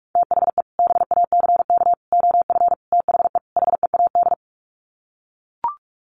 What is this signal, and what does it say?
Morse code, heard as a keyed tone, about 44 wpm, 700 Hz. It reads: THE BACK OF THE HEAD
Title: Sets of 5 Words Encore2 45wpm